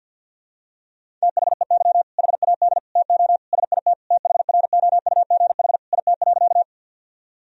Morse code, heard as a keyed tone, about 47 wpm, 700 Hz. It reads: THEY HAD TO SIT THROUGH IT.
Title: Sentences from Top 1000 Words - Review Part 1 - 50wpm